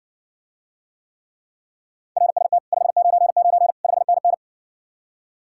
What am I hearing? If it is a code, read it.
Morse code, heard as a keyed tone, about 60 wpm, 700 Hz. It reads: RST 599 5NN